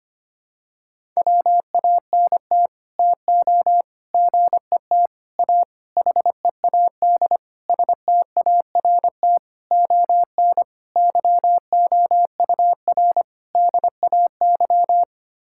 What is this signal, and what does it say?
Morse code, heard as a keyed tone, about 25 wpm, 700 Hz. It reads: WANT TO GET A HEAD START ON YOUR DAY
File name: Sets of 9 Words - 25wpm